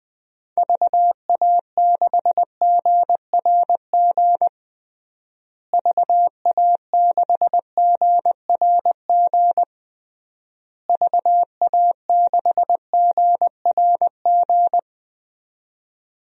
Morse code, heard as a keyed tone, about 20 wpm, 700 Hz. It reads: VA6GRG VA6GRG VA6GRG
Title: DXCC-100 Call Signs - 6 Characters - Repeated 3x - 20wpm